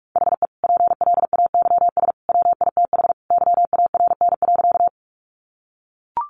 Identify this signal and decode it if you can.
Morse code, heard as a keyed tone, about 44 wpm, 700 Hz. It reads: HE PLAYS WITH YARN.
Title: Easy Phrases 45wpm